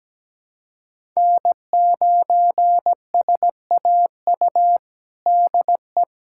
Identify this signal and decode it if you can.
Morse code, heard as a keyed tone, about 17 wpm, 700 Hz. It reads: N9SAU DE
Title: USA Callsign DE Callsign - 5 Characters - 17wpm